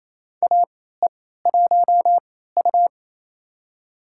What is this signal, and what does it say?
Morse code, heard as a keyed tone, about 27 wpm, spaced as for 16 wpm, 700 Hz. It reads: AE1U
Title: USA Call Signs - 3x Character Spacing - 28wpm